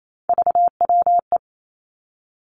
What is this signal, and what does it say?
Morse code, heard as a keyed tone, about 28 wpm, 700 Hz. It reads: VWE
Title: Three Challenging Letter-Number Combinations 28wpm